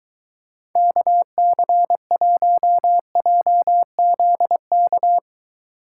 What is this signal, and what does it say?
Morse code, heard as a keyed tone, about 21 wpm, 700 Hz. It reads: KC1JZK